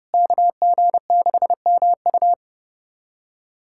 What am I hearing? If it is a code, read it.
Morse code, heard as a keyed tone, about 30 wpm, 700 Hz. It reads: KG6MU
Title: USA Callsign DE Callsign - 5 Characters - 30wpm